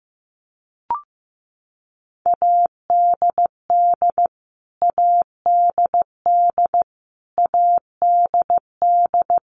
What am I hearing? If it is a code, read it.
Morse code, heard as a keyed tone, about 15 wpm, 700 Hz. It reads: ADD ADD ADD
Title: Top 500 Words Repeated 15wpm